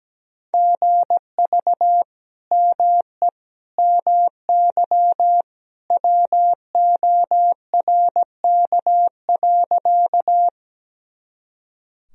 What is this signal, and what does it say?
Morse code, heard as a keyed tone, about 17 wpm, 700 Hz. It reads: GV ME MY WORK.
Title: Sentences from Top 200 Words and Abbreviations 17wpm